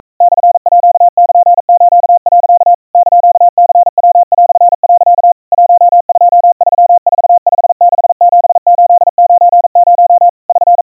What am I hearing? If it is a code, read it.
Morse code (Japanese Wabun code), heard as a keyed tone, about 42 wpm, 700 Hz. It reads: モールス、 <KN>ニヤ)、 1234567890 <SN>
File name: kawa / Morse sample / iroha 140 cpm 700 Hz Crispy